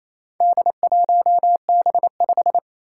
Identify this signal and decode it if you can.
Morse code, heard as a keyed tone, about 27 wpm, 700 Hz. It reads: D1B5
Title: Four Challenging Letter-Number Combinations 28wpm